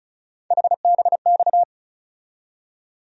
Morse code, heard as a keyed tone, about 36 wpm, 700 Hz. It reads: HBX
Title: USA Call Sign Postfixes - 3 Letters 35wpm